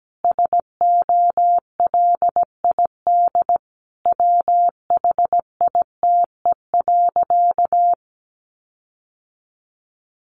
Morse code, heard as a keyed tone, about 17 wpm, 700 Hz. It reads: SOLID WHITE.
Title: Sentences from Top 1500 Words 17wpm